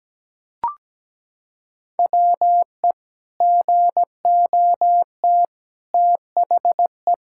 Morse code, heard as a keyed tone, about 17 wpm, 700 Hz. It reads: WE GOT THE
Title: Sentences from Top 1000 Words - Review Part 5 - 17wpm